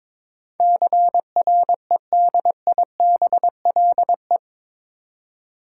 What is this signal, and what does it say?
Morse code, heard as a keyed tone, about 21 wpm, 700 Hz. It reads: CREDIBLE
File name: practice / 22wpm / Eight Letter Words - 22wpm